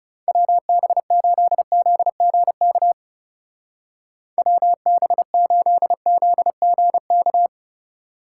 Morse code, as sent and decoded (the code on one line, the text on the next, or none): .-- -... ---.. --.. --. -.- / .-- -... ---.. --.. --. -.-
WB8ZGK WB8ZGK